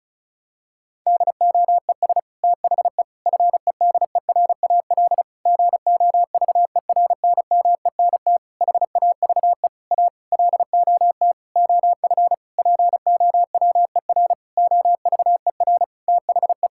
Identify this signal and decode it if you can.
Morse code, heard as a keyed tone, about 34 wpm, 700 Hz. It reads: DOES THE FEDERAL GOVERNMENT HAVE A LOT OF POWER OVER THE
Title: Sentences from Top 1000 Words - Review Part 5 - 35wpm